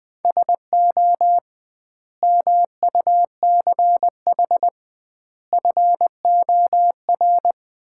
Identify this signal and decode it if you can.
Morse code, heard as a keyed tone, about 20 wpm, 700 Hz. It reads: SO MUCH FOR
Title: 2x Word-Spacing - Sets of 5 Words - 20wpm